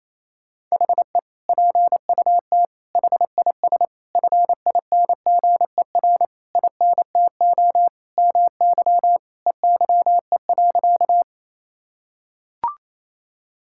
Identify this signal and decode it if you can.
Morse code, heard as a keyed tone, about 27 wpm, 700 Hz. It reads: HE PUT HIS FINGER INTO MY EYE.
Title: Sentences from Top 1600 Words 28wpm